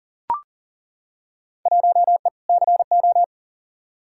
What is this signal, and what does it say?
Morse code, heard as a keyed tone, about 40 wpm, 700 Hz. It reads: 1E CO
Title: Field Day Exchanges - 40wpm